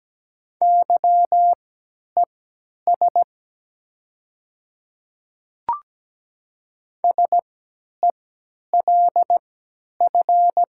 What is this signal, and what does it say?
Morse code, heard as a keyed tone, about 17 wpm, spaced as for 10 wpm, 700 Hz. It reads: YES SELF
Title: 3x Character Spacing - Top 500 Words - 17wpm